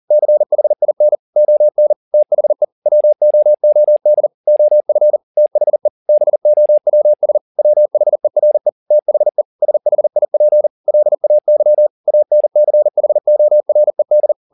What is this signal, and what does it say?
Morse code, heard as a keyed tone, about 40 wpm, 600 Hz. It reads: CHIN ON THE WOOD OF THE BOWS WHERE THE SHIP LAY ANCHORED
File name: the old man and the sea 2 40wpm